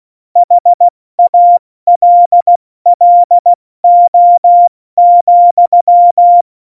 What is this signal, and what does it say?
Morse code, heard as a keyed tone, about 16 wpm, 700 Hz. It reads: HALLO,
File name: Morsekurs / Witze / Witz 18. Februar 2020, 70 F80